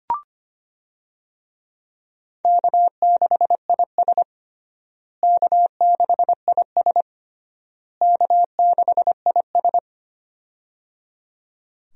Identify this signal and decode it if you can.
Morse code, heard as a keyed tone, about 25 wpm, 700 Hz. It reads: K6IS K6IS K6IS
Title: USA Call Signs - 4 Characters - Repeated 3x - 25wpm